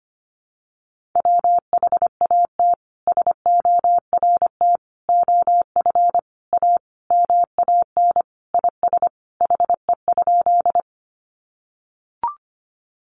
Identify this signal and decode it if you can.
Morse code, heard as a keyed tone, about 25 wpm, 700 Hz. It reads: WHAT SORT OF A MAN IS HE?
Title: Sentences from Top 1300 Words 25wpm